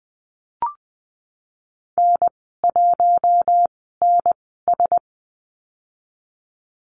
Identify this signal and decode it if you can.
Morse code, heard as a keyed tone, about 20 wpm, spaced as for 15 wpm, 700 Hz. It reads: N1NS
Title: 042 Lesson 8g - Number 1 - Call Signs 20wpm